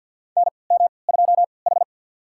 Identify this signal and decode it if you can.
Morse code, heard as a keyed tone, about 51 wpm, spaced as for 28 wpm, 700 Hz. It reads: NM2H
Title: USA Call Signs - 3x Character Spacing - 50wpm